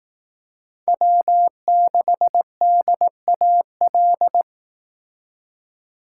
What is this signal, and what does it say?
Morse code, heard as a keyed tone, about 18 wpm, 700 Hz. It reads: W6DAL
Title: Speed Racing - USA 1x3 Call Signs 15wpm